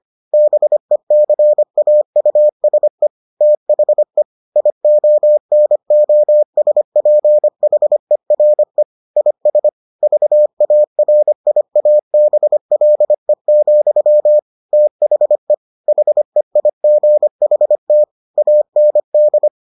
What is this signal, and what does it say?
Morse code, heard as a keyed tone, about 25 wpm, 600 Hz. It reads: BECAUSE THE IONOSPHERE IS VARIABLE, THE HEIGHT AND